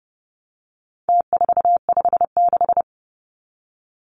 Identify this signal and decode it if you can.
Morse code, heard as a keyed tone, about 30 wpm, 700 Hz. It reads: T456
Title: Numbers - Contest Style - 4-Digit 30wpm